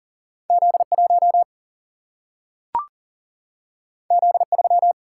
Morse code, heard as a keyed tone, about 40 wpm, 700 Hz. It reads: Z1 Z3
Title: Single-Letter Single-Number 40wpm